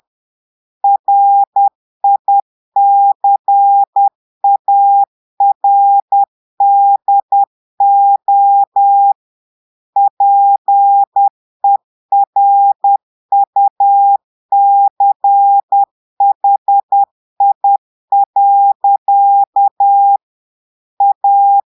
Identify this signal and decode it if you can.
Morse code, heard as a keyed tone, about 10 wpm, 800 Hz. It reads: RICARDO PERUCHI. A
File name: Portugais - texte en clair 1 - vitesse - 10 mots minute